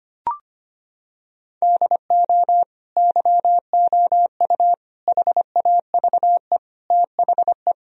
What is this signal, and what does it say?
Morse code, heard as a keyed tone, about 25 wpm, 700 Hz. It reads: DO YOU HAVE THE
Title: Sentences from Top 1000 Words - Review Part 3 - 25wpm